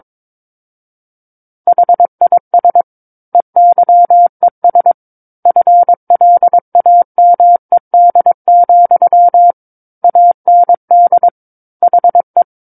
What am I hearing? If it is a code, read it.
Morse code, heard as a keyed tone, about 22 wpm, 700 Hz. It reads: HIS EYES FLAMED, AND HE
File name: Book One / 13-Chapter 13 - 22 WPM